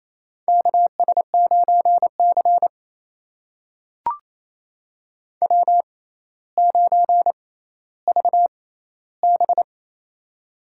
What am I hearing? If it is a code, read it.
Morse code, heard as a keyed tone, about 28 wpm, spaced as for 14 wpm, 700 Hz. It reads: KS9C W9VB
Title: USA Call Signs - 4 Characters - 6x Character Spacing - 28wpm